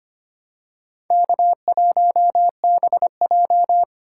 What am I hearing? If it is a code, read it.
Morse code, heard as a keyed tone, about 25 wpm, 700 Hz. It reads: K1BJ
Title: USA Call Signs - 4 Characters - 3x Character Spacing - 25wpm